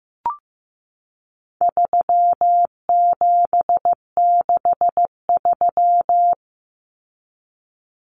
Morse code, heard as a keyed tone, about 15 wpm, 700 Hz. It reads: 3763
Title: Common Cut Numbers - 4-Digit 15wpm